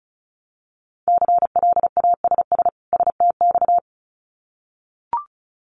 Morse code, heard as a keyed tone, about 34 wpm, 700 Hz. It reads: CLASS STX
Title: Winter Field Day Exchanges 35wpm